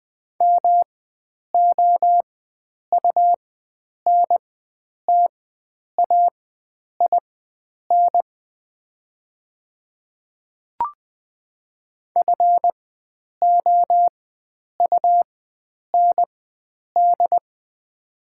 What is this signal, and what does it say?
Morse code, heard as a keyed tone, about 20 wpm, spaced as for 9 wpm, 700 Hz. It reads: MOUNTAIN FOUND
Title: 4x Character Spacing - Top 500 Words - 20wpm